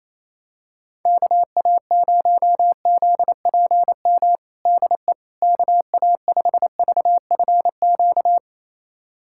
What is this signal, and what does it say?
Morse code, heard as a keyed tone, about 28 wpm, 700 Hz. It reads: KA0ZPM DE KA5VFQ
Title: USA Callsign DE Callsign - 6 Characters - 28wpm